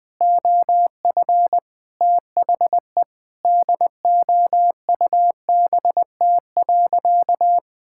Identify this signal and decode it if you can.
Morse code, heard as a keyed tone, about 20 wpm, 700 Hz. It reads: OF THE DOUBT.
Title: Sentences with 7-Word Phrases - 20wpm